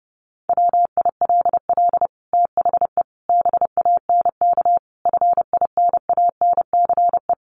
Morse code, heard as a keyed tone, about 30 wpm, 700 Hz. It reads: WILL THE BANK FINANCE
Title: Sentences from Top 1500 Words 30wpm